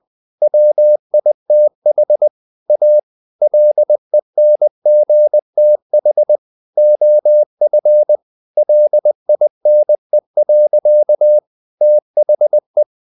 Morse code, heard as a keyed tone, about 20 wpm, 600 Hz. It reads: WITH A LENGTH OF LINE. THE
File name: the old man and the sea part 2 20wpm